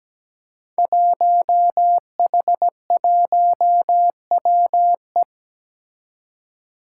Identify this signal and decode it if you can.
Morse code, heard as a keyed tone, about 17 wpm, 700 Hz. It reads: 1H1WE
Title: Five Challenging Letter-Number Combinations 17wpm